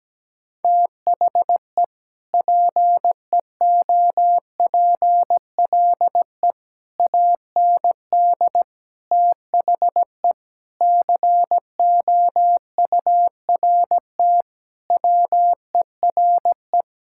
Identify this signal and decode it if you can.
Morse code, heard as a keyed tone, about 17 wpm, 700 Hz. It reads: THE PEOPLE AND THE COURT WERE
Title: Sentences from Top 1600 Words 17wpm